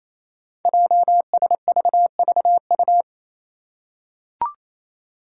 Morse code, heard as a keyed tone, about 27 wpm, 700 Hz. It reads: JSVVU
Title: Five Challenging Letter-Number Combinations 28wpm